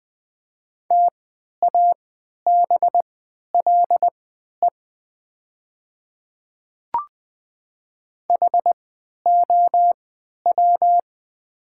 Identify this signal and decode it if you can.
Morse code, heard as a keyed tone, about 20 wpm, spaced as for 11 wpm, 700 Hz. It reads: TABLE HOW
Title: 3x Character Spacing - Top 500 Words - 20wpm